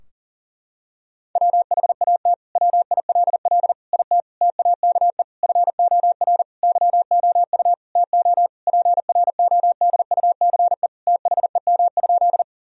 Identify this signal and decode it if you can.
Morse code, heard as a keyed tone, about 40 wpm, 700 Hz. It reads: WHAT WILL IT TAKE FOR YOU TO PRODUCE THEM?
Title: Sentences from Top 1000 Words - Review Part 3 - 40wpm